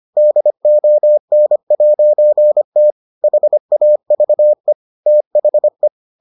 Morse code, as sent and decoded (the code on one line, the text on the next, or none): -.. --- -. .----. - / .... .- ...- . / - .... .
DON'T HAVE THE